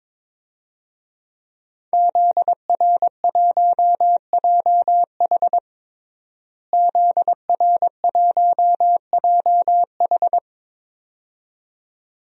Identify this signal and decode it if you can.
Morse code, heard as a keyed tone, about 22 wpm, 700 Hz. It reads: ZR1JH ZR1JH
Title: DXCC-100 Call Signs - 5 Characters - Repeated 2x - 22wpm